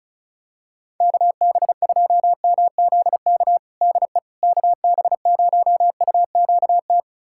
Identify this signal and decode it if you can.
Morse code, heard as a keyed tone, about 36 wpm, 700 Hz. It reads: KB2MZK DE KB0UQT
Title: USA Callsign DE Callsign 35wpm